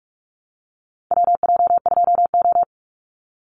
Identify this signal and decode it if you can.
Morse code, heard as a keyed tone, about 44 wpm, 700 Hz. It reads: WJ2O
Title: Call Signs Repeated 3x - 45wpm